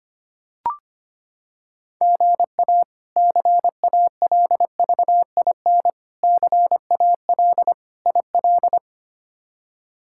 Morse code, heard as a keyed tone, about 25 wpm, 700 Hz. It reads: GA CALVIN CAL IL
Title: K1USN SST QSO Practice - 25wpm